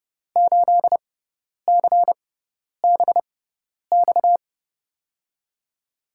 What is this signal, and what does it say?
Morse code, heard as a keyed tone, about 31 wpm, spaced as for 10 wpm, 700 Hz. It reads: 8CBX
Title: USA Call Signs - 5 Characters - 6x Character Spacing - 30wpm